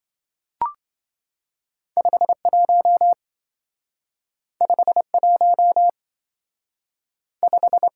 Morse code, heard as a keyed tone, about 25 wpm, 700 Hz. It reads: 51 51 5
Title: Speed-Racing - Two Letter-Number Combinations - 20wpm